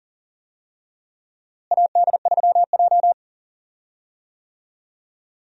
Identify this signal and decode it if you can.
Morse code, heard as a keyed tone, about 40 wpm, 700 Hz. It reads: AD3J
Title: USA Call Signs - 4 Characters - Repeated 3x - 40wpm